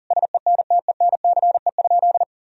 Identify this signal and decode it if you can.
Morse code, heard as a keyed tone, about 37 wpm, 700 Hz. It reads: SENTENCE?
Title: Sentences from Top 1000 Words - Review Part 1 - 40wpm